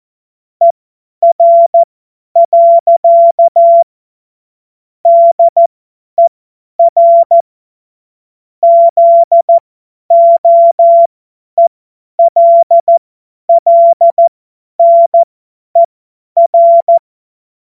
Morse code, heard as a keyed tone, about 14 wpm, spaced as for 10 wpm, 680 Hz. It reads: ER. DER ZOELLNER
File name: Morsekurs / Witze / Witz 07.November 2023 50 F70